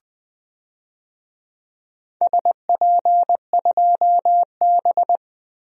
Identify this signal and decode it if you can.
Morse code, heard as a keyed tone, about 20 wpm, 700 Hz. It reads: SP2B
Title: DXCC-100 Call Signs - 4 Characters - Repeated 2x - 20wpm